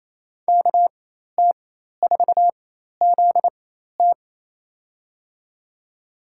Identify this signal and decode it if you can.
Morse code, heard as a keyed tone, about 28 wpm, spaced as for 13 wpm, 700 Hz. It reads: KT4ZT
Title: USA Call Signs - 5 Characters - 4x Character Spacing - 28wpm